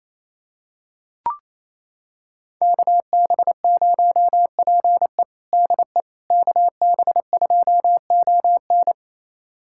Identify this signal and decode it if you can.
Morse code, heard as a keyed tone, about 28 wpm, 700 Hz. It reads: KB0PE DE KB2ON
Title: USA Callsign DE Callsign - 5 Characters - 28wpm